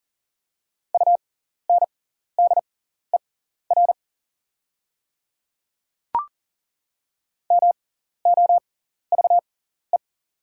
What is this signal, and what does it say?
Morse code, heard as a keyed tone, about 37 wpm, spaced as for 14 wpm, 700 Hz. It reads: UNDER MOVE
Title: 6x Character Spacing - Top 300 Words - 40wpm